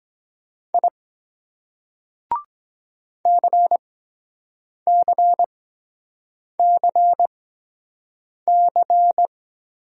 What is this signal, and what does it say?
Morse code, heard as a keyed tone, about 23 wpm, spaced as for 7 wpm, 700 Hz. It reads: I CCCC